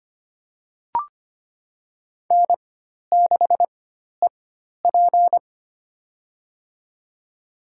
Morse code, heard as a keyed tone, about 25 wpm, spaced as for 12 wpm, 700 Hz. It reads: N6EP
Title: USA Call Signs - 4 Characters - 4x Character Spacing - 25wpm